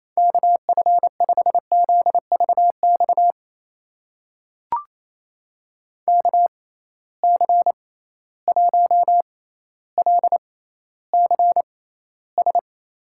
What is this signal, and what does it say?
Morse code, heard as a keyed tone, about 28 wpm, spaced as for 14 wpm, 700 Hz. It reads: KF5ZVX KC1LCS
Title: USA Call Signs - 6 Characters - 6x Character Spacing - 28wpm